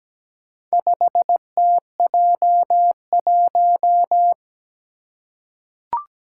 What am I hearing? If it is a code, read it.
Morse code, heard as a keyed tone, about 17 wpm, 700 Hz. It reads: 5TJ1
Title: Four Challenging Letter-Number Combinations 17wpm